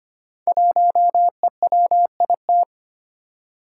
Morse code, heard as a keyed tone, about 25 wpm, 700 Hz. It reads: 1EWIT
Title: Five Challenging Letter-Number Combinations 25wpm